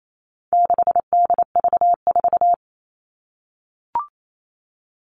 Morse code, heard as a keyed tone, about 28 wpm, 700 Hz. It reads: 6DV4
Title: All Cut Numbers - 4-Digit 28wpm